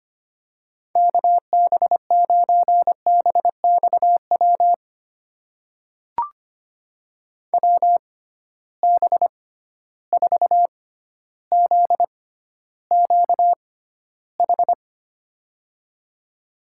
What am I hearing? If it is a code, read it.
Morse code, heard as a keyed tone, about 25 wpm, spaced as for 13 wpm, 700 Hz. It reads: KB9BXW WB4ZQH
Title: USA Call Signs - 6 Characters - 6x Character Spacing - 25wpm